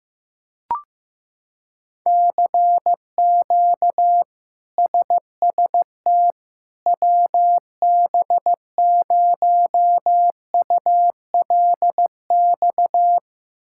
Morse code, heard as a keyed tone, about 15 wpm, 700 Hz. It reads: CQ SST WB0ULX
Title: K1USN SST QSO Practice - 15wpm